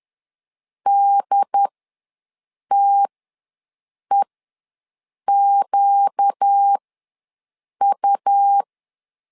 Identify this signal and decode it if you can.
Morse code, heard as a keyed tone, about 10 wpm, spaced as for 6 wpm, 800 Hz. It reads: DTEQU
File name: LEZIONI / 4 Q ws 10wpm 2306